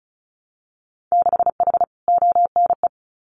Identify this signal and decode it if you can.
Morse code, heard as a keyed tone, about 36 wpm, 700 Hz. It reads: 6H ONE
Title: Winter Field Day Exchanges Encore 35wpm